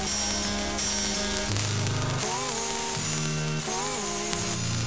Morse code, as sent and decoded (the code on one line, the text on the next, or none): none